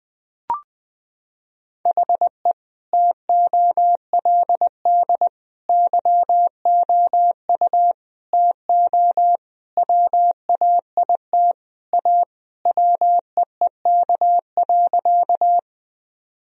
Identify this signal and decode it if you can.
Morse code, heard as a keyed tone, about 20 wpm, 700 Hz. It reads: HE TOLD YOU TO WAIT A WEEK.
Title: Sentences from Top 1000 Words - Review Part 2 - 20wpm